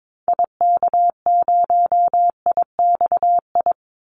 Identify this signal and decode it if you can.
Morse code, heard as a keyed tone, about 22 wpm, 700 Hz. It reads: IK0IXI
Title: Call Signs Repeated 3x - 22wpm